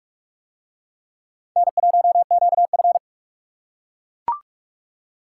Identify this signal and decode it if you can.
Morse code, heard as a keyed tone, about 40 wpm, 700 Hz. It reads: N1QF